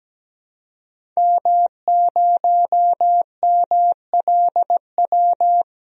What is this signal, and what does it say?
Morse code, heard as a keyed tone, about 17 wpm, 700 Hz. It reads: M0MLW
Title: DXCC-100 Call Signs - 5 Characters - 17wpm